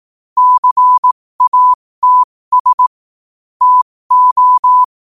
Morse code, heard as a keyed tone, about 18 wpm, spaced as for 15 wpm, 1000 Hz. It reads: CATS TO
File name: qotdfiles / 15wpm